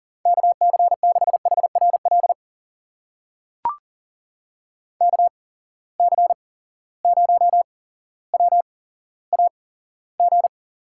Morse code, heard as a keyed tone, about 40 wpm, spaced as for 17 wpm, 700 Hz. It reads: KC6HRL KC0WAG
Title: USA Call Signs - 6 Characters - 8x Character Spacing - 40wpm